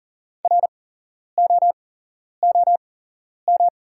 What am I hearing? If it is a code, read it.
Morse code, heard as a keyed tone, about 40 wpm, spaced as for 11 wpm, 700 Hz. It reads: ROOM
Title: 8x Character Spacing - Top 500 Words - 40wpm